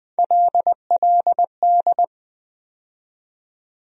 Morse code, heard as a keyed tone, about 20 wpm, 700 Hz. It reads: LLD
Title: USA Call Sign Postfixes - 3 Letters 20wpm